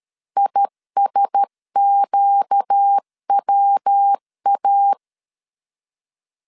Morse code, heard as a keyed tone, about 13 wpm, 800 Hz. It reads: ISQWA
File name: CW / DET / 1 6 1500C S C 12wpm 3533